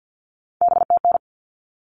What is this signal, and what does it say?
Morse code, heard as a keyed tone, about 50 wpm, 700 Hz. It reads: BTN
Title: All Cut Numbers - 3-Digit 50wpm